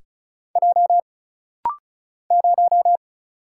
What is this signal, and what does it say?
Morse code, heard as a keyed tone, about 36 wpm, 700 Hz. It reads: J 0
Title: Single Letter-Number - Rapid-Fire 35wpm